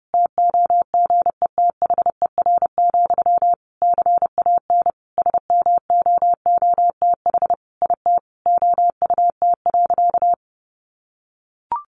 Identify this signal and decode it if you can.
Morse code, heard as a keyed tone, about 30 wpm, 700 Hz. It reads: TOGETHER, CAN SMOOTH IT OUT.
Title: Sentences from Top 1400 Words 30wpm